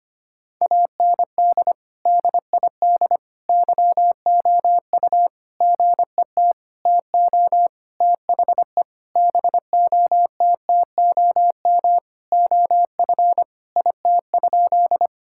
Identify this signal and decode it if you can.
Morse code, heard as a keyed tone, about 25 wpm, 700 Hz. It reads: AND DID YOU GET TO THE BOTTOM OF IT?
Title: Sentences from Top 1000 Words - Review Part 6 - 25wpm